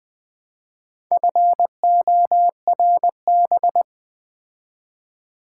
Morse code, heard as a keyed tone, about 20 wpm, 690 Hz. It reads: FORB
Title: Build A Word - 2 Letters At A Time - 6-Letter Words - 20wpm